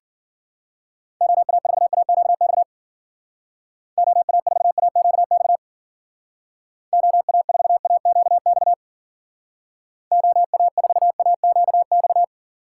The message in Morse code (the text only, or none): OA4AQX OA4AQX OA4AQX OA4AQX